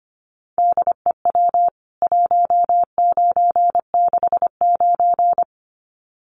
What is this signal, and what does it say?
Morse code, heard as a keyed tone, about 25 wpm, 700 Hz. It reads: DEW 1969
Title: CWT Exchange 25wpm